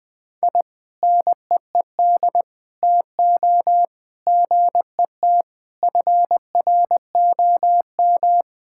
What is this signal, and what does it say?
Morse code, heard as a keyed tone, about 21 wpm, 700 Hz. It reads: I NEED TO GET FROM
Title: Sentences with 7-Word Phrases - 20wpm